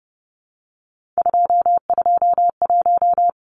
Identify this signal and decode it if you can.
Morse code, heard as a keyed tone, about 30 wpm, 700 Hz. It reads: 221
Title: Numbers - 3-Digit 30wpm